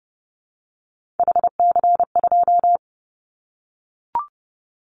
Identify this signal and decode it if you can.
Morse code, heard as a keyed tone, about 30 wpm, 700 Hz. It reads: HC2
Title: Callsign Prefixes for Top 100 DXCC - 30wpm